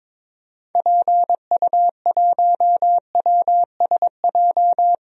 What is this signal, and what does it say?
Morse code, heard as a keyed tone, about 22 wpm, 700 Hz. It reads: PU1WSJ
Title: DXCC-100 Call Signs - 6 Characters - Repeated 3x - 22wpm